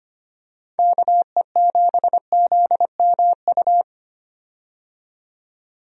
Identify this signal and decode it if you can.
Morse code, heard as a keyed tone, about 25 wpm, 700 Hz. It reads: KE7ZMU
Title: USA Call Signs - 6 Characters - 25wpm